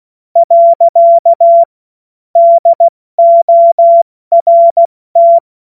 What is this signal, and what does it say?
Morse code, heard as a keyed tone, about 16 wpm, 680 Hz. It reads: . DORT
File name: Morsekurs / Witze / Witz 24.Oktober 2023 70 F80